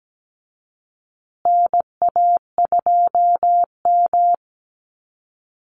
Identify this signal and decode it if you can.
Morse code, heard as a keyed tone, about 17 wpm, 700 Hz. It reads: NA2M
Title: Call Signs Repeated 2x - 17wpm